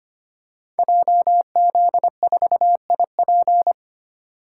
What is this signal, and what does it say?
Morse code, heard as a keyed tone, about 25 wpm, 700 Hz. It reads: JZ4IP
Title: Code Groups - 5 Letter:Numbers - 25wpm